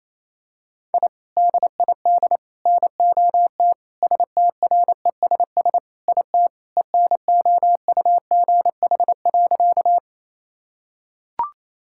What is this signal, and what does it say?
Morse code, heard as a keyed tone, about 27 wpm, 700 Hz. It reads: I DID NOT STRESS IT ENOUGH.
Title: Sentences from Top 1000 Words - Review Part 6 - 28wpm